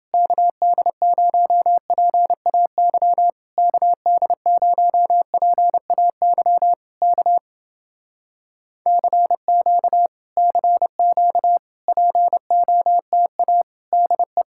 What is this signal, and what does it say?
Morse code, heard as a keyed tone, about 29 wpm, 700 Hz. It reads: KD0PAY KD0PAY K CQ CQ POTA DE